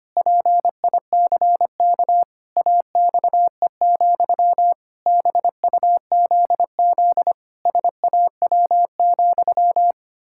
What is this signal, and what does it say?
Morse code, heard as a keyed tone, about 25 wpm, 700 Hz. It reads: PICK AXE, BUZZ SAW,
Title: Pangram Challenge v2 - 25wpm